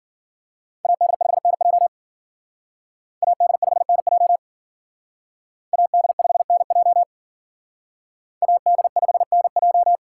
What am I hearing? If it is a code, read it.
Morse code, heard as a keyed tone, about 47 wpm, 700 Hz. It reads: AD5NJ AD5NJ AD5NJ AD5NJ
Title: Speed Racing - USA 2x2 Call Signs 40wpm